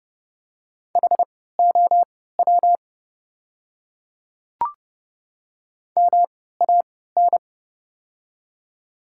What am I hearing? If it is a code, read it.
Morse code, heard as a keyed tone, about 30 wpm, spaced as for 17 wpm, 700 Hz. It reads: HOW MAN
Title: 3x Character Spacing - Top 300 Words - 30wpm